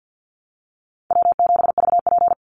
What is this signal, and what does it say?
Morse code, heard as a keyed tone, about 50 wpm, 700 Hz. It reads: W7VP
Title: Call Signs Repeated 2x - 50wpm